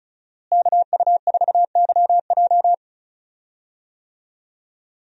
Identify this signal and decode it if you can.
Morse code, heard as a keyed tone, about 34 wpm, 700 Hz. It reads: KU4YJ
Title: USA Call Signs - 5 Characters - Repeated 2x - 35wpm